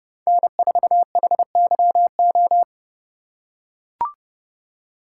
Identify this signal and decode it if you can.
Morse code, heard as a keyed tone, about 28 wpm, 700 Hz. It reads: N4HYO